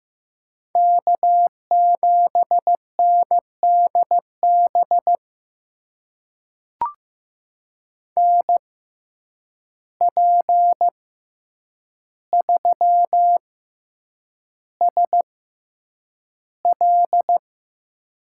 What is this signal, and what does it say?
Morse code, heard as a keyed tone, about 15 wpm, spaced as for 8 wpm, 700 Hz. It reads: K7NDB NP3SL
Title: USA Call Signs - 5 Characters - 6x Character Spacing - 15wpm